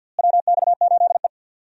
Morse code, heard as a keyed tone, about 49 wpm, 700 Hz. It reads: WX8E
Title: USA Call Signs - 4 Characters - Repeated 2x - 50wpm